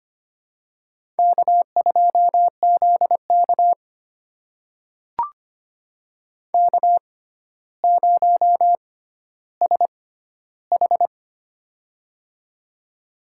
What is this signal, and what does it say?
Morse code, heard as a keyed tone, about 25 wpm, spaced as for 13 wpm, 700 Hz. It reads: K2ZK K0SH
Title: USA Call Signs - 4 Characters - 6x Character Spacing - 25wpm